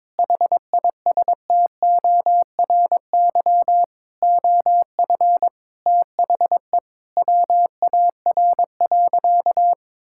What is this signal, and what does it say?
Morse code, heard as a keyed tone, about 22 wpm, 700 Hz. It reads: HISTORY OF THE WAR.
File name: practice / 22wpm / Sentences with 4-Word Phrases - 22wpm